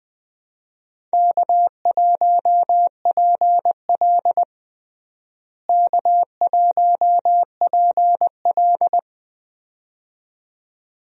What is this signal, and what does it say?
Morse code, heard as a keyed tone, about 21 wpm, 700 Hz. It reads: K1PL K1PL
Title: USA Call Signs - 4 Characters - Repeated 2x - 20wpm